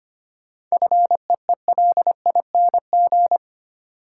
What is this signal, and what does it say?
Morse code, heard as a keyed tone, about 25 wpm, 700 Hz. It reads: FEELING
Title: Sentences from Top 1200 Words - New Word Drill Alt - 25wpm